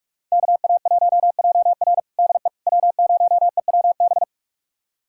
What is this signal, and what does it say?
Morse code, heard as a keyed tone, about 41 wpm, 700 Hz. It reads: KA1JR DE W0EWB
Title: USA Callsign DE Callsign - 5 Characters - 45wpm